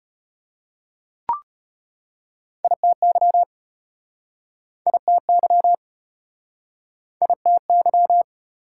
Morse code, heard as a keyed tone, about 34 wpm, 700 Hz. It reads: ITY ITY ITY